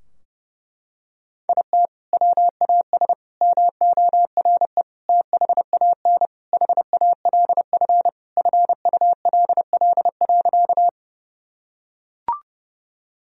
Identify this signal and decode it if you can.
Morse code, heard as a keyed tone, about 30 wpm, 700 Hz. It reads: IT WAS MORE THAN HALF FULL.
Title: Sentences from Top 1000 Words - Review Part 3 - 30wpm